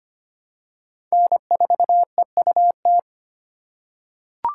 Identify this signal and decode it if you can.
Morse code, heard as a keyed tone, about 25 wpm, 700 Hz. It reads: N4EUT
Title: USA Call Signs - 4x Character Spacing - 25wpm